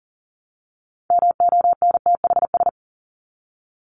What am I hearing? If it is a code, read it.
Morse code, heard as a keyed tone, about 40 wpm, 700 Hz. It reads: MONTHS
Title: ES:S Suffix - Top 100 Words with Base Words 40wpm